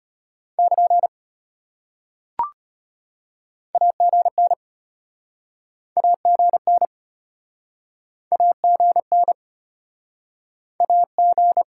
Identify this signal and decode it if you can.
Morse code, heard as a keyed tone, about 36 wpm, 700 Hz. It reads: <KN> AGN AGN AGN AG